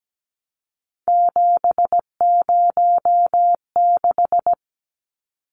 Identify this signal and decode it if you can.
Morse code, heard as a keyed tone, about 17 wpm, 700 Hz. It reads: 706
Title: Numbers - 3-Digit 17wpm